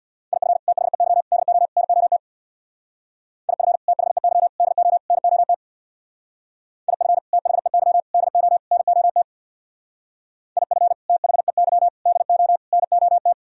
Morse code, heard as a keyed tone, about 64 wpm, 700 Hz. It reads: IF THEY DO NOT IF THEY DO NOT IF THEY DO NOT IF THEY DO NOT